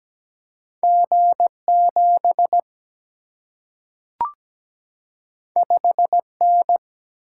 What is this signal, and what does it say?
Morse code, heard as a keyed tone, about 17 wpm, 700 Hz. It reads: G7 5N
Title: Single-Letter Single-Number 17wpm